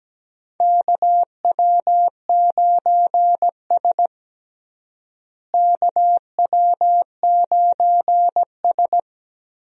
Morse code, heard as a keyed tone, about 17 wpm, 700 Hz. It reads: KW9S KW9S
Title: USA Call Signs - 4 Characters - Repeated 2x - 17wpm